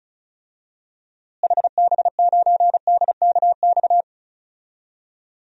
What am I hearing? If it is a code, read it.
Morse code, heard as a keyed tone, about 36 wpm, 700 Hz. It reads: HB9DKX
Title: DXCC-100 Call Signs - 6 Characters - Repeated 3x - 35wpm